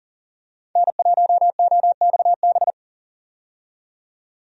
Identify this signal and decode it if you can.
Morse code, heard as a keyed tone, about 37 wpm, 700 Hz. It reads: N1OXB
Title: USA Call Signs - 5 Characters - 40wpm